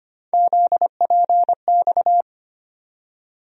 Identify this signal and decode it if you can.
Morse code, heard as a keyed tone, about 25 wpm, 700 Hz. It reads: ZPX
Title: USA Call Sign Postfixes - 3 Letters 25wpm